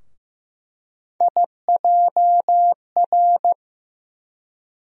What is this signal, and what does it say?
Morse code, heard as a keyed tone, about 15 wpm, 700 Hz. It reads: IJR
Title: USA Call Sign Postfixes - 3 Letters 15wpm